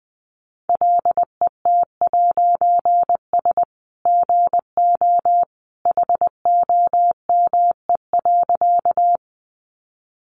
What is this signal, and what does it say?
Morse code, heard as a keyed tone, about 20 wpm, 700 Hz. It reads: LET'S GO HOME.
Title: A Sound Of Thunder 20wpm